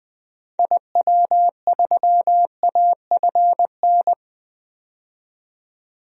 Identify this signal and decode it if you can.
Morse code, heard as a keyed tone, about 20 wpm, 700 Hz. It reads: IW3AFN
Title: DXCC-100 Call Signs - 6 Characters - 20wpm